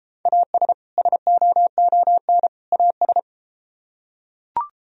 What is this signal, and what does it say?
Morse code, heard as a keyed tone, about 33 wpm, 700 Hz. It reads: AS SOON AS